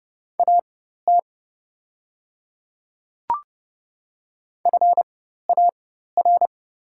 Morse code, heard as a keyed tone, about 29 wpm, spaced as for 14 wpm, 690 Hz. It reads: AT FAR
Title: 4x Character Spacing - Top 500 Words - 30wpm